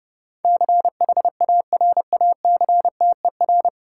Morse code, heard as a keyed tone, about 32 wpm, 700 Hz. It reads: CHARACTER
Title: Sentences from Top 1200 Words - New Word Drill - 30wpm